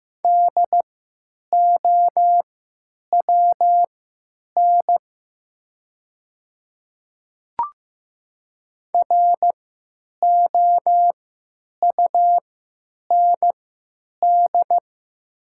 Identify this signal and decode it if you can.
Morse code, heard as a keyed tone, about 15 wpm, spaced as for 9 wpm, 700 Hz. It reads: DOWN ROUND
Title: 3x Character Spacing - Top 200 Words - 15wpm